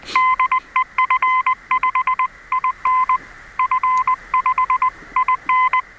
Morse code, heard as a keyed tone, about 20 wpm, 1030 Hz. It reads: DE F5IN F5IN
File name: attachments / F5IN 21mHz RX UA1CEG p 9.03.2016